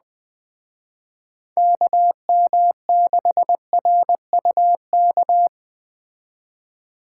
Morse code, heard as a keyed tone, about 20 wpm, 700 Hz. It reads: KM6RUK